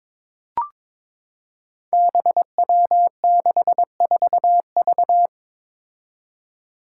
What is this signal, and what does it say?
Morse code, heard as a keyed tone, about 22 wpm, 700 Hz. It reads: BW64V
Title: Five Challenging Letter-Number Combinations 22wpm